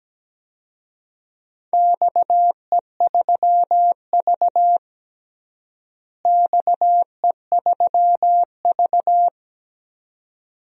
Morse code, heard as a keyed tone, about 17 wpm, 700 Hz. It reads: XE3V XE3V
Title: DXCC-100 Call Signs - 4 Characters - Repeated 2x - 17wpm